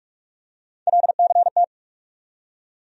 Morse code, heard as a keyed tone, about 48 wpm, 700 Hz. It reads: LCT
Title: USA Call Sign Postfixes - 3 Letters 45wpm